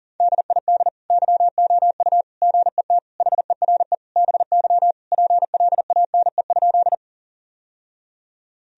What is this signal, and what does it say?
Morse code, heard as a keyed tone, about 40 wpm, 700 Hz. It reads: DID YOU GET HERE BY PLANE?
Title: Sentences from Top 1000 Words - Review Part 3 - 40wpm